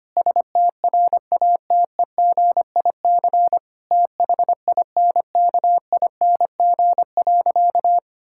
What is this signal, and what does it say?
Morse code, heard as a keyed tone, about 25 wpm, 700 Hz. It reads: STRATEGIC THINKING.
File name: practice / 25wpm / Difficult Sentences with Hints - 25wpm